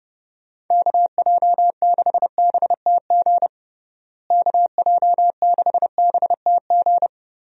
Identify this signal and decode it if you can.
Morse code, heard as a keyed tone, about 30 wpm, 700 Hz. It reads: KJ6BTG KJ6BTG
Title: USA Call Signs - 6 Characters - Repeated 3x - 30wpm